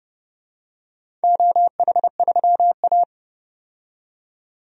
Morse code, heard as a keyed tone, about 32 wpm, 700 Hz. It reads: OH3A
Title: DXCC-100 Call Signs - 4 Characters - 30wpm